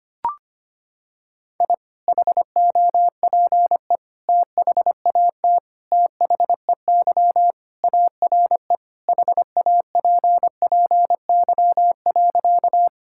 Morse code, heard as a keyed tone, about 25 wpm, 700 Hz. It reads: I HOPE THAT THEY ARE HAPPY.
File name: practice / 25wpm / Sentences with 3-Word Phrases - 25wpm